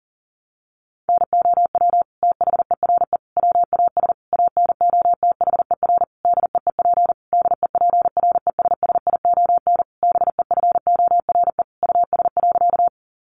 Morse code, heard as a keyed tone, about 40 wpm, 700 Hz. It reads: NOW THERE WAS ANOTHER DEEP DEPRESSION BEFORE US.
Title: Sentences from Top 1300 Words 40wpm